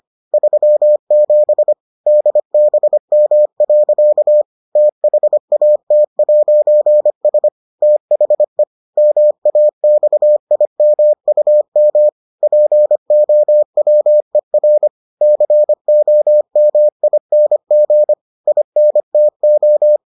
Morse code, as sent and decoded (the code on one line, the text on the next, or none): ...-- --... / -.. -... -- .-.-.- / - .... .- - .----. ... / - .... / -- .- -..- .. -- ..- -- / .--. --- .-- . .-. / -.-. --- -- .. -. --. / .. -. - ---
37 DBM. THAT'S THE MAXIMUM POWER COMING INTO